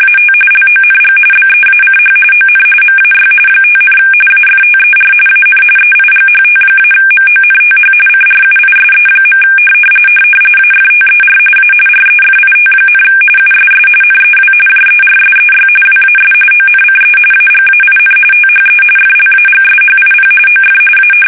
RTTY, about 45.45 baud. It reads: AT LONG BEACH ON MAY 5, AFTER AN EIGHT-MONTH CRUISE. AFTER A REFIT, USS NEW JERSEY WAS ONCE AGAIN THE FLAGSHIP OF A
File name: fsk850 / battleshipnj